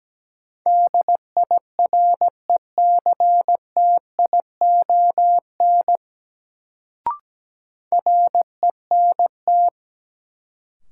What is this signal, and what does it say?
Morse code, heard as a keyed tone, about 17 wpm, 700 Hz. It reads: DIRECTION RENT
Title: Sentences from Top 1000 Words - Review Part 5 - New Word Drill - 17wpm